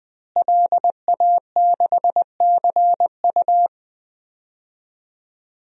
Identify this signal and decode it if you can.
Morse code, heard as a keyed tone, about 20 wpm, 700 Hz. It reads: LA6CU
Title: DXCC-100 Call Signs - 5 Characters - Repeated 3x - 20wpm